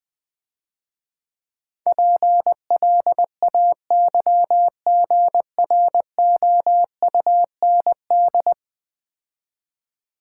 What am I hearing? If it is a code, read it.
Morse code, heard as a keyed tone, about 20 wpm, 700 Hz. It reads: PLAYGROUND
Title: Easy Words - Speed-Racing with Hints - 20wpm